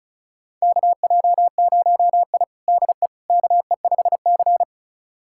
Morse code, heard as a keyed tone, about 34 wpm, 700 Hz. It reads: KJ0I DE KE5C
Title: USA Callsign DE Callsign - 4 Characters - 35wpm